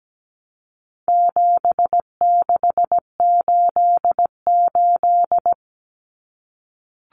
Morse code, heard as a keyed tone, about 17 wpm, 700 Hz. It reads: 7688
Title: Common Cut Numbers - 4-Digit 17wpm